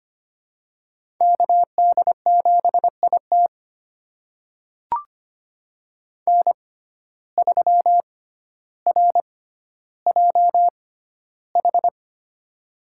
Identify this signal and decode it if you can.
Morse code, heard as a keyed tone, about 25 wpm, spaced as for 13 wpm, 700 Hz. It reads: KD7IT N3RJH
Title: USA Call Signs - 5 Characters - 6x Character Spacing - 25wpm